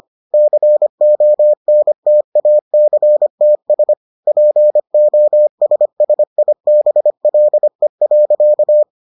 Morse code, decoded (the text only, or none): CONTACTS POSSIBLE.